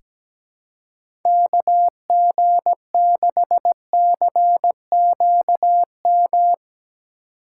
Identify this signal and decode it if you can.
Morse code, heard as a keyed tone, about 17 wpm, 700 Hz. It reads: KG6CQM
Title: USA Call Signs - 6 Characters - 4x Character Spacing - 17wpm